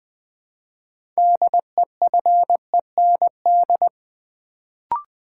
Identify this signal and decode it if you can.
Morse code, heard as a keyed tone, about 20 wpm, 700 Hz. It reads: DEFEND
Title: Build A Word - 3 Letters At A Time - 6-Letter Words - 20wpm